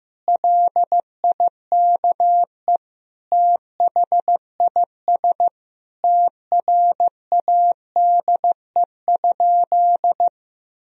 Morse code, heard as a keyed tone, about 15 wpm, 700 Hz. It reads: LIKE THIS TRADE?
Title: Sentences from Top 1000 Words - Review Part 4 - 15wpm